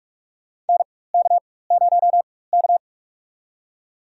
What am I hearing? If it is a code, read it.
Morse code, heard as a keyed tone, about 45 wpm, spaced as for 21 wpm, 700 Hz. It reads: NK0K
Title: USA Call Signs - 4 Characters - 4x Character Spacing - 45wpm